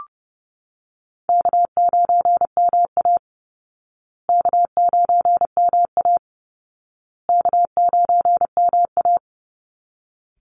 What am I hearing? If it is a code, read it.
Morse code, heard as a keyed tone, about 30 wpm, 700 Hz. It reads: K9MA K9MA K9MA
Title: Call Signs Repeated 2x - 30wpm